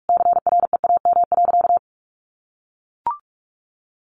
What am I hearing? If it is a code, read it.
Morse code, heard as a keyed tone, about 44 wpm, 700 Hz. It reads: CREAM.
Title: Sentences from Top 1500 Words 45wpm